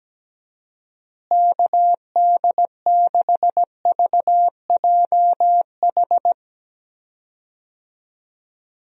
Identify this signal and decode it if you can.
Morse code, heard as a keyed tone, about 17 wpm, 700 Hz. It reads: KD6VJH
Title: USA Call Signs - 6 Characters - Repeated 2x - 17wpm